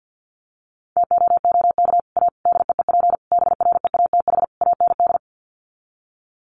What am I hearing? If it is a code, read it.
Morse code, heard as a keyed tone, about 49 wpm, 700 Hz. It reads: TOOK A DEEP BREATH AND
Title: Sets of 5 Words Encore 50wpm